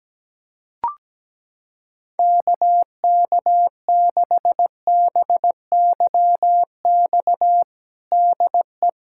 Morse code, heard as a keyed tone, about 17 wpm, 700 Hz. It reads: KK6BYX DE
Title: USA Callsign DE Callsign - 6 Characters - 17wpm